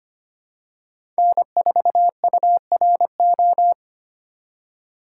Morse code, heard as a keyed tone, about 25 wpm, 700 Hz. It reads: N4URO
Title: USA Call Signs - 5 Characters - 4x Character Spacing - 25wpm